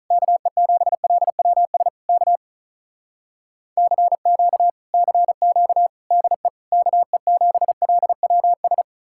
Morse code, heard as a keyed tone, about 47 wpm, 700 Hz. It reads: KE7LWS K CQ CQ DE KE7LWS